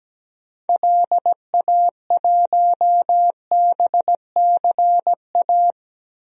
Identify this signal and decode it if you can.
Morse code, heard as a keyed tone, about 17 wpm, 700 Hz. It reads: LA1BCA
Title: DXCC-100 Call Signs - 6 Characters - 17wpm